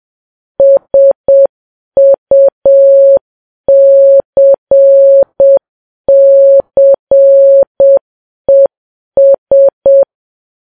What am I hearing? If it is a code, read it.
Morse code, heard as a keyed tone, about 7 wpm, 550 Hz. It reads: SUCCES